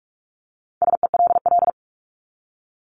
Morse code, heard as a keyed tone, about 44 wpm, 700 Hz. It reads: SELL
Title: Top 1001-1250 Words 45wpm